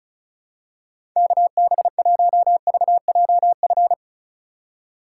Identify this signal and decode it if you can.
Morse code, heard as a keyed tone, about 36 wpm, 700 Hz. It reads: KB1VJF
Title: USA Call Signs - 6 Characters - 4x Character Spacing - 35wpm